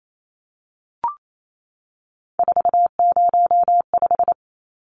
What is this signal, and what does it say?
Morse code, heard as a keyed tone, about 28 wpm, 700 Hz. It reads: 405
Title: Numbers - 3-Digit 28wpm